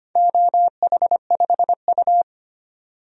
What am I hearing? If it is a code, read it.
Morse code, heard as a keyed tone, about 25 wpm, 700 Hz. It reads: OH5U
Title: DXCC-100 Call Signs - 4 Characters - 25wpm